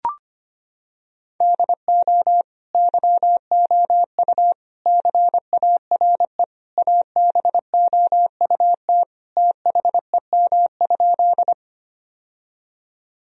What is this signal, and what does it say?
Morse code, heard as a keyed tone, about 25 wpm, 700 Hz. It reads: DO YOU CARE ABOUT THEM?
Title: Sentences from Top 1000 Words - Review Part 1 - 25wpm